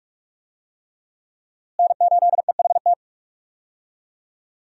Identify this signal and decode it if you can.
Morse code, heard as a keyed tone, about 42 wpm, 700 Hz. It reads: N8EHT
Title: USA Call Signs - 5 Characters - Repeated 3x - 45wpm